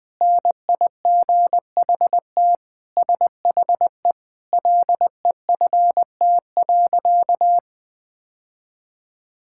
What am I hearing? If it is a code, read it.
Morse code, heard as a keyed tone, about 21 wpm, 700 Hz. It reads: NIGHT SHE LEFT.
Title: Sentences with 3-Word Phrases - Spoken Beginning - 20wpm